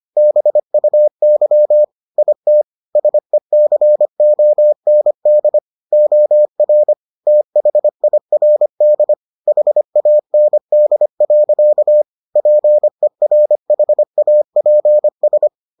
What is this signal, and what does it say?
Morse code, heard as a keyed tone, about 25 wpm, 600 Hz. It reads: BUY IT SECOND OR THIRD HAND. PERHAPS